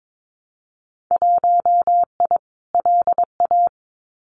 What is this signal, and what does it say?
Morse code, heard as a keyed tone, about 22 wpm, 700 Hz. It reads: 1I LA
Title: Winter Field Day Exchanges Encore 22wpm